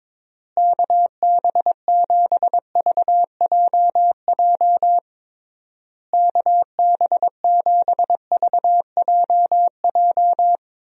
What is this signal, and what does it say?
Morse code, heard as a keyed tone, about 22 wpm, 700 Hz. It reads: KB7VJJ KB7VJJ
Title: USA Call Signs - 6 Characters - Repeated 3x - 22wpm